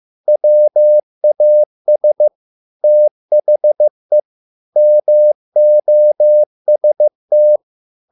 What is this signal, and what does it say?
Morse code, heard as a keyed tone, about 15 wpm, 600 Hz. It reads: WAS THE MOST